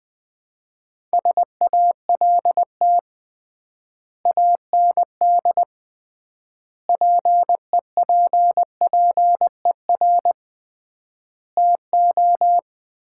Morse code, heard as a keyed tone, about 20 wpm, 700 Hz. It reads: SALT AND PEPPER TO
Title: 3x Word-Spacing - Sets of 5 Words - 20wpm